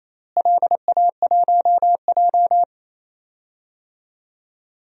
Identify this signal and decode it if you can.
Morse code, heard as a keyed tone, about 28 wpm, 700 Hz. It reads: LA1J
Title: DXCC-100 Call Signs - 4 Characters - Repeated 3x - 28wpm